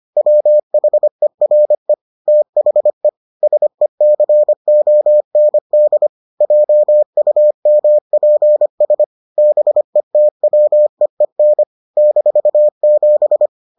Morse code, heard as a keyed tone, about 25 wpm, 600 Hz. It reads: WHERE THE SECOND JUMPS BETWEEN -7